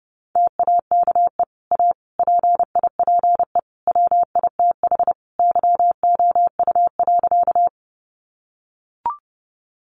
Morse code, heard as a keyed tone, about 30 wpm, 700 Hz. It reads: TAKE A PIPE WITH YOU.
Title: Sentences from Top 1500 Words 30wpm